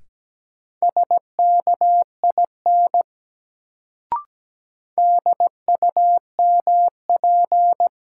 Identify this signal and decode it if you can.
Morse code, heard as a keyed tone, about 17 wpm, 700 Hz. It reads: SKIN DUMP
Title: Sentences from Top 1300 Words - New Word Drill Alt - 17wpm